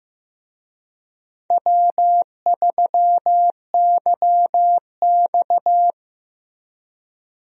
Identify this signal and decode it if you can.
Morse code, heard as a keyed tone, about 15 wpm, 700 Hz. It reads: W3YX
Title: USA Call Signs - 4 Characters - Repeated 3x - 15wpm